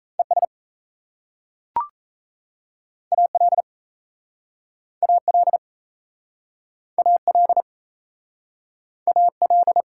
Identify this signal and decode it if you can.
Morse code, heard as a keyed tone, about 37 wpm, 700 Hz. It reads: ES AL AL AL AL